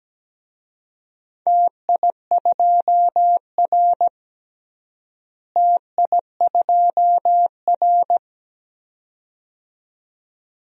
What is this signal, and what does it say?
Morse code, heard as a keyed tone, about 17 wpm, 700 Hz. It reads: TI2R TI2R
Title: DXCC-100 Call Signs - 4 Characters - Repeated 2x - 17wpm